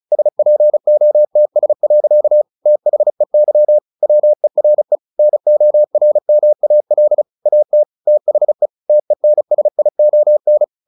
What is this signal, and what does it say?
Morse code, heard as a keyed tone, about 34 wpm, 600 Hz. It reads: SPOTS. THEY WERE NORMAL AT THE TENSION
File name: the old man and the sea 2 35wpm